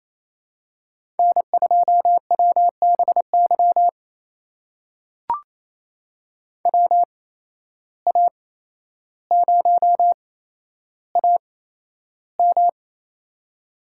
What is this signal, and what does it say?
Morse code, heard as a keyed tone, about 27 wpm, spaced as for 12 wpm, 700 Hz. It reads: N2WBY WA0AM
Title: USA Call Signs - 5 Characters - 8x Character Spacing - 28wpm